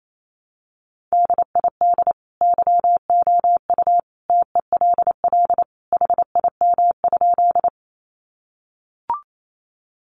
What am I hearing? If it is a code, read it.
Morse code, heard as a keyed tone, about 28 wpm, 700 Hz. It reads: DID YOU TELL HIM?
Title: Sentences from Top 200 Words 28wpm